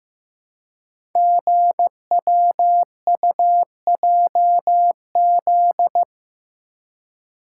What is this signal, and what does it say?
Morse code, heard as a keyed tone, about 15 wpm, 700 Hz. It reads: GWUJZ
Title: Code Groups - 5 Letters - 15wpm